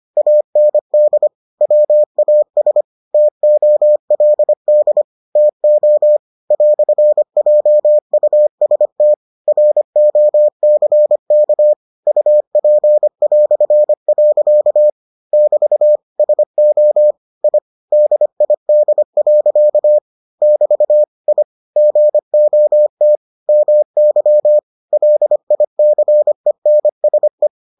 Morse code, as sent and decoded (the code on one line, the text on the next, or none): .- -. -.. / .-- .- ... / - --- .-.. -.. / - --- / .-..-. .--- ..- ... - / .-. --- -.-. -.- / ..- .--. .-..-. .-.-.- / -...- / ... --- / .. / -.. .. -.. .-.-.- / -...- / .. / --. --- - / -- -.-- / .-.. .. -.-. -. ... .
AND WAS TOLD TO "JUST ROCK UP". <BT> SO I DID. <BT> I GOT MY LICENSE